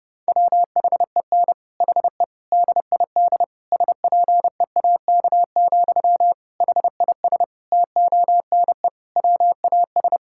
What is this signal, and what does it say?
Morse code, heard as a keyed tone, about 30 wpm, 700 Hz. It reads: WHEN HE DID SPEAK, HIS TONE WAS
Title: Sentences from Top 1600 Words 30wpm